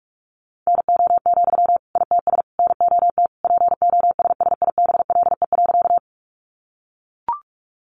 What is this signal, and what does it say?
Morse code, heard as a keyed tone, about 44 wpm, 700 Hz. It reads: NO, ITS NOT POSSIBLE.
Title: Sentences from Top 600 Words 45wpm